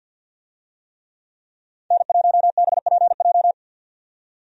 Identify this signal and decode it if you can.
Morse code, heard as a keyed tone, about 47 wpm, 700 Hz. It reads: N1BPJ
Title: USA Call Signs - 5 Characters - 50wpm